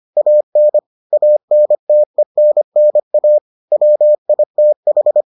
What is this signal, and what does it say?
Morse code, heard as a keyed tone, about 25 wpm, 600 Hz. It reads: AN ANTENNA WITH